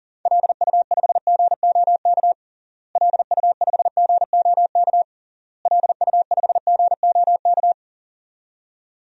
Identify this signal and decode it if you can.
Morse code, heard as a keyed tone, about 40 wpm, 700 Hz. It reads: LU5GOK LU5GOK LU5GOK
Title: DXCC-100 Call Signs - 6 Characters - Repeated 3x - 40wpm